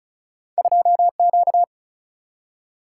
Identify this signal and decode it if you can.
Morse code, heard as a keyed tone, about 34 wpm, 700 Hz. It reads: 2 Q
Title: Single-Letter Single-Number 35wpm